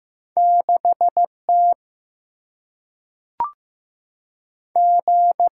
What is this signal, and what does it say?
Morse code, heard as a keyed tone, about 15 wpm, 700 Hz. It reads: 6 T G
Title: Single-Letter Single-Number 15wpm